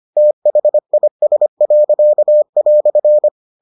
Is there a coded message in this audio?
Yes: Morse code from 0.2 to 3.3 s, heard as a keyed tone, about 25 wpm, 600 Hz. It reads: THIS."